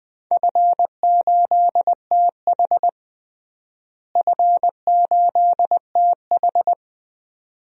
Morse code, heard as a keyed tone, about 20 wpm, 700 Hz. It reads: F8TH F8TH
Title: DXCC-100 Call Signs - 4 Characters - Repeated 2x - 20wpm